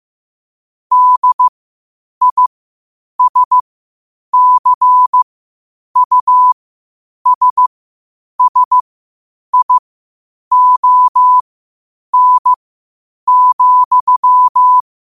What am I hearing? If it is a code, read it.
Morse code, heard as a keyed tone, about 15 wpm, spaced as for 8 wpm, 1000 Hz. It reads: DISCUSSION,
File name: qotdfiles / 8wpm